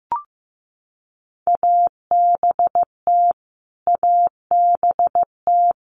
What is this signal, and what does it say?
Morse code, heard as a keyed tone, about 15 wpm, 700 Hz. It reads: ABT ABT
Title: Abbreviations Repeated 3x - 15wpm